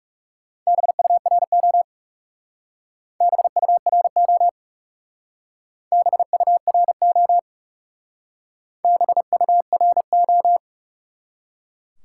Morse code, heard as a keyed tone, about 38 wpm, 700 Hz. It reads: BURO BURO BURO BURO